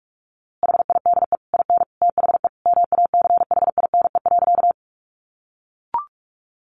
Morse code, heard as a keyed tone, about 44 wpm, 700 Hz. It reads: HIDE IN THE MACHINE.
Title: A Sound Of Thunder 45wpm